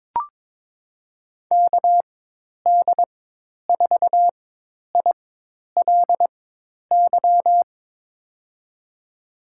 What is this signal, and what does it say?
Morse code, heard as a keyed tone, about 22 wpm, spaced as for 10 wpm, 700 Hz. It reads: KD4ILY
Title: USA Call Signs - 4x Character Spacing - 22wpm